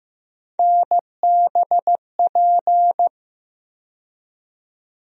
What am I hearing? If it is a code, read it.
Morse code, heard as a keyed tone, about 15 wpm, 700 Hz. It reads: NBP
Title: USA Call Sign Postfixes - 3 Letters 15wpm